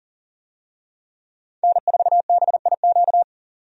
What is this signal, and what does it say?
Morse code, heard as a keyed tone, about 39 wpm, 700 Hz. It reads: N4BIQ
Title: USA Call Signs - 5 Characters - 40wpm